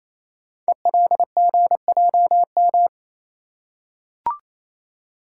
Morse code, heard as a keyed tone, about 27 wpm, 700 Hz. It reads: ELGJM
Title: Code Groups - 5 Letters - 28wpm